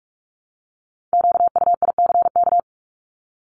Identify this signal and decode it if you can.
Morse code, heard as a keyed tone, about 44 wpm, 700 Hz. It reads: QUICK
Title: LY Suffix - Top 100 Words with Base Words 45wpm